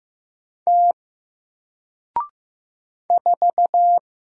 Morse code, heard as a keyed tone, about 15 wpm, 700 Hz. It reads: T 4
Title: Single Letter-Number Alt 15wpm